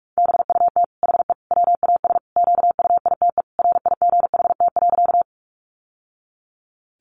Morse code, heard as a keyed tone, about 44 wpm, 700 Hz. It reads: BUT HE WAS QUITE RIGHT.
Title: Sentences from Top 700 Words 45wpm